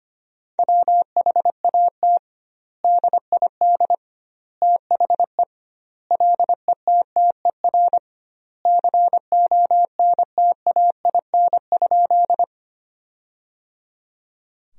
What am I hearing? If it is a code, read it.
Morse code, heard as a keyed tone, about 25 wpm, 700 Hz. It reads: WHAT DID THE LETTER CONTAIN?
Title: 2x Word-Spacing - Sentences from Top 500 Words - 25wpm